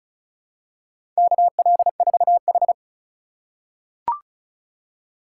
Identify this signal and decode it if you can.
Morse code, heard as a keyed tone, about 34 wpm, 700 Hz. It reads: KL4H
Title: USA Call Signs - 4 Characters - 2x Character Spacing - 35wpm